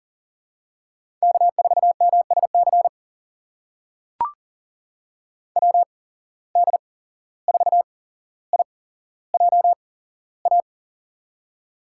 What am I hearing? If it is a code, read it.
Morse code, heard as a keyed tone, about 40 wpm, spaced as for 16 wpm, 700 Hz. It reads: K4MSC WD4IJA
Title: USA Call Signs - 8x Character Spacing - 40wpm